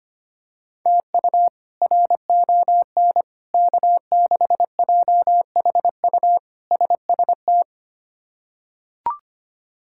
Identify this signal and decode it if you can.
Morse code, heard as a keyed tone, about 25 wpm, 700 Hz. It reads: TU RON K6JHU SST
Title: K1USN SST QSO Practice - 25wpm